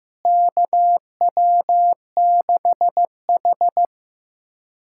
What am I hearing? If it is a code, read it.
Morse code, heard as a keyed tone, about 16 wpm, 700 Hz. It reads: KW6H
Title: USA Callsign DE Callsign - 4 Characters - 15wpm